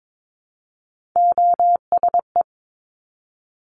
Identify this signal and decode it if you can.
Morse code, heard as a keyed tone, about 22 wpm, 700 Hz. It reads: OSE
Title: Top Three Letter Combinations in Words 22wpm